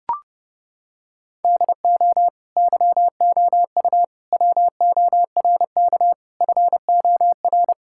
Morse code, heard as a keyed tone, about 30 wpm, 700 Hz. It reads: DO YOU WORK FOR
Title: Sentences from Top 1000 Words - Review Part 5 - 30wpm